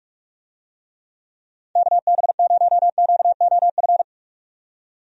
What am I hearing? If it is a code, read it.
Morse code, heard as a keyed tone, about 46 wpm, 700 Hz. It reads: KB0QOF
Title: USA Call Signs - 6 Characters - 45wpm